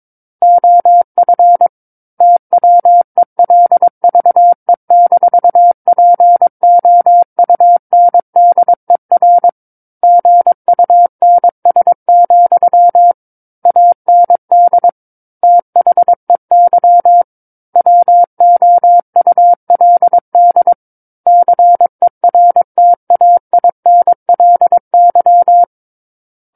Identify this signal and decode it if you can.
Morse code, heard as a keyed tone, about 22 wpm, 700 Hz. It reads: OF TWELVE-POUNDER GUNS, AND THEY WOULD CERTAINLY
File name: Book One / 13-Chapter 13 - 22 WPM